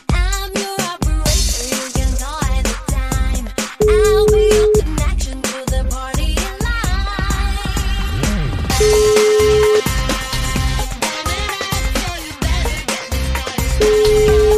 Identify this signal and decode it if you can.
Ringback tone (Europe, 425 Hz), with noise.